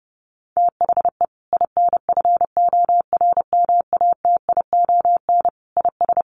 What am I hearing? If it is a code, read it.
Morse code, heard as a keyed tone, about 30 wpm, 700 Hz. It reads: THE INFORMATION IS
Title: Sentences from Top 1100 Words 30wpm